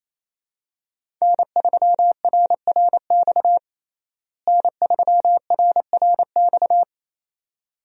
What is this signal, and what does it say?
Morse code, heard as a keyed tone, about 28 wpm, 700 Hz. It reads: N3RRX N3RRX
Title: USA Call Signs - 5 Characters - Repeated 2x - 28wpm